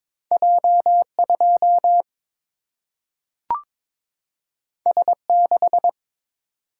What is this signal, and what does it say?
Morse code, heard as a keyed tone, about 22 wpm, 700 Hz. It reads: J2 S6
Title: Single-Letter Single-Number 22wpm